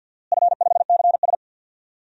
Morse code, heard as a keyed tone, about 51 wpm, 700 Hz. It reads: F5CS
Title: DXCC-100 Call Signs - 4 Characters - Repeated 3x - 50wpm